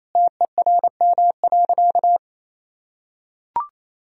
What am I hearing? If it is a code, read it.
Morse code, heard as a keyed tone, about 28 wpm, 700 Hz. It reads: TERM.
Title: Sentences from Top 1000 Words - Review Part 4 - 28wpm